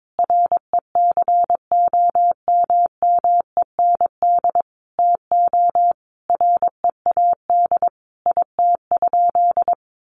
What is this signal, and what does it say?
Morse code, heard as a keyed tone, about 22 wpm, 700 Hz. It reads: RECOMMEND TO READ IT?
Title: Sentences from Top 1200 Words 22wpm